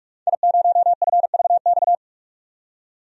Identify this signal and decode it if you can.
Morse code, heard as a keyed tone, about 42 wpm, 700 Hz. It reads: I0FVX
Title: DXCC-100 Call Signs - 5 Characters - 45wpm